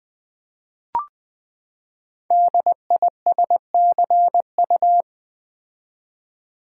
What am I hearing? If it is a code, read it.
Morse code, heard as a keyed tone, about 20 wpm, 700 Hz. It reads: DISCU
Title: Build A Word - 5 Letters At A Time - 10-Letter Words - 20wpm